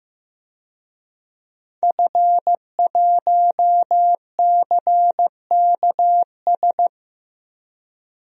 Morse code, heard as a keyed tone, about 15 wpm, 700 Hz. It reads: F1CKS
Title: DXCC-100 Call Signs - 5 Characters - Repeated 3x - 15wpm